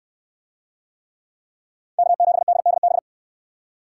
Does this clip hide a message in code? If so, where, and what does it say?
Yes: Morse code from 2.0 to 3.0 s, heard as a keyed tone, about 68 wpm, 700 Hz. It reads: K7DDB